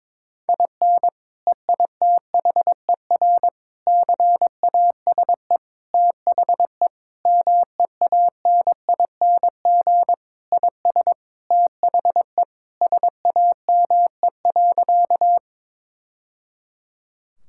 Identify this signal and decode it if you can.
Morse code, heard as a keyed tone, about 22 wpm, 700 Hz. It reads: IN EITHER CASE THE MEANING IS THE SAME.
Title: Sentences from Top 1000 Words - Review Part 6 - 22wpm